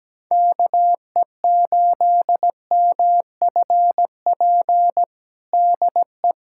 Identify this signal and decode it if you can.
Morse code, heard as a keyed tone, about 17 wpm, 700 Hz. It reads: KE8MFP DE
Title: USA Callsign DE Callsign - 6 Characters - 17wpm